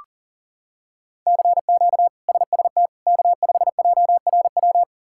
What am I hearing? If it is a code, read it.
Morse code, heard as a keyed tone, about 40 wpm, 700 Hz. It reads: CQ SST K5JRW
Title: K1USN SST QSO Practice - 40wpm